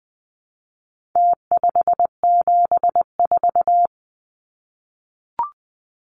Morse code, heard as a keyed tone, about 20 wpm, 700 Hz. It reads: T574
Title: Numbers - Contest Style - 4-Digit 20wpm